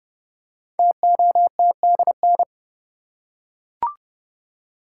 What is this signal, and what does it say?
Morse code, heard as a keyed tone, about 30 wpm, 700 Hz. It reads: TOTDN
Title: Code Groups - 5 Letters - 30wpm